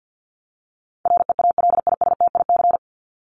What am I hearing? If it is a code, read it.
Morse code, heard as a keyed tone, about 50 wpm, 700 Hz. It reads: REALISTIC
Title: Top 1751-2000 Words 50wpm